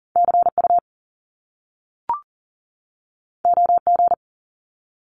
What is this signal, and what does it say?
Morse code, heard as a keyed tone, about 43 wpm, 700 Hz. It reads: CU OG
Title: Top Two Letter Combinations in Words 40wpm